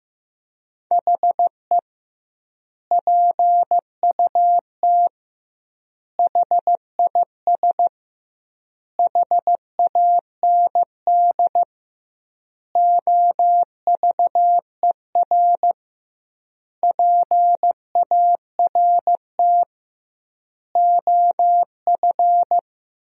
Morse code, heard as a keyed tone, about 15 wpm, 700 Hz. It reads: HE PUT HIS HAND OVER PART OF
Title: 2x Word-Spacing - Sentences from Top 400 Words - 15wpm